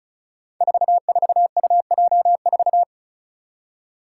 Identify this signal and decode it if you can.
Morse code, heard as a keyed tone, about 34 wpm, 700 Hz. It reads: 44UJ4
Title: Five Challenging Letter-Number Combinations 35wpm